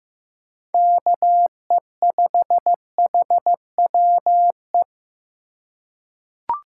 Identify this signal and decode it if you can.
Morse code, heard as a keyed tone, about 16 wpm, 700 Hz. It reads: KE5HWE
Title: USA Call Signs - 6 Characters - 3x Character Spacing - 15wpm